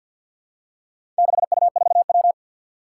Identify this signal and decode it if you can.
Morse code, heard as a keyed tone, about 51 wpm, 700 Hz. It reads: 6U4W
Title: Four Challenging Letter-Number Combinations 50wpm